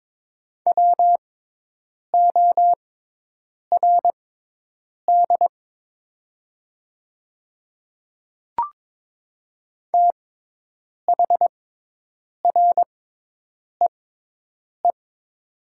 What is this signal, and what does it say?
Morse code, heard as a keyed tone, about 22 wpm, spaced as for 8 wpm, 700 Hz. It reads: WORD THREE
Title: 6x Character Spacing - Top 500 Words - 22wpm